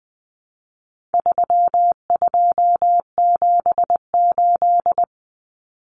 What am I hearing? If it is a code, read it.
Morse code, heard as a keyed tone, about 20 wpm, 700 Hz. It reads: 3278
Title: Common Cut Numbers - 4-Digit 20wpm